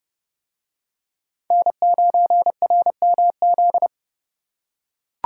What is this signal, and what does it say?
Morse code, heard as a keyed tone, about 28 wpm, 700 Hz. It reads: N9RMZ